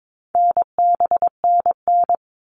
Morse code, heard as a keyed tone, about 22 wpm, 700 Hz. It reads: NBNN
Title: All Cut Numbers - 4-Digit 22wpm